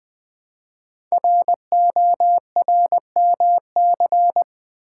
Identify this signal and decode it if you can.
Morse code, heard as a keyed tone, about 20 wpm, 700 Hz. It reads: RORMC
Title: Code Groups - 5 Letters - 20wpm